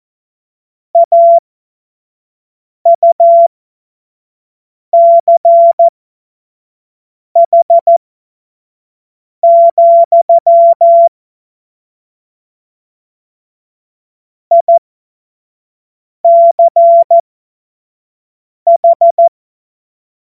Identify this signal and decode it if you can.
Morse code, heard as a keyed tone, about 14 wpm, spaced as for 5 wpm, 680 Hz. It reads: AUCH, ICH
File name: Morsekurs / Witze / Witz 07. März 2023 25 F70